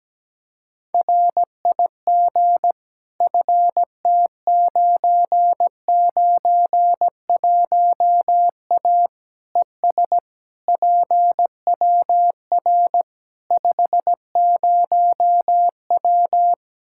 Morse code, heard as a keyed tone, about 17 wpm, 700 Hz. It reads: RIG FT991A ES PWR 50W
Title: K1USN SST QSO Practice 17wpm